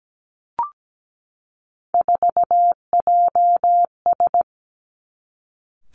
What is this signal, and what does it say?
Morse code, heard as a keyed tone, about 17 wpm, 700 Hz. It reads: 4JS
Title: Three Challenging Letter-Number Combinations 17wpm